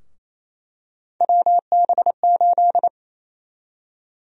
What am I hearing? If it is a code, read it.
Morse code, heard as a keyed tone, about 27 wpm, 700 Hz. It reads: WB8
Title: USA Call Sign Prefixes - 28wpm